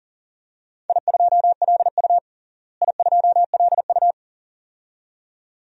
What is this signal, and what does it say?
Morse code, heard as a keyed tone, about 40 wpm, 700 Hz. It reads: I2LU I2LU
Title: DXCC-100 Call Signs - 4 Characters - Repeated 2x - 40wpm